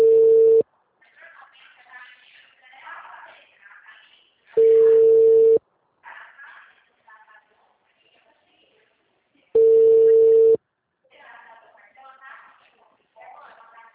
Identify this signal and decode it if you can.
Ringback tone (China, 450 Hz), with speech.